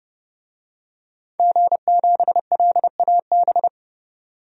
Morse code, heard as a keyed tone, about 30 wpm, 700 Hz. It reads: G7LAB
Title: DXCC-100 Call Signs - 5 Characters - 30wpm